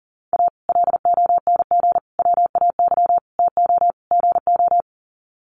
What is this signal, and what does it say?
Morse code, heard as a keyed tone, about 37 wpm, 700 Hz. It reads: A LONG WAY TO GO
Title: Sets of 5 Words 40wpm